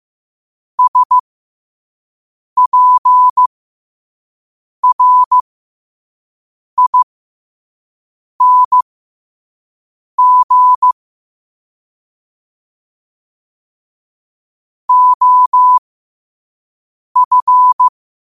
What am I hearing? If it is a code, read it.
Morse code, heard as a keyed tone, about 15 wpm, spaced as for 5 wpm, 1000 Hz. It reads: SPRING OF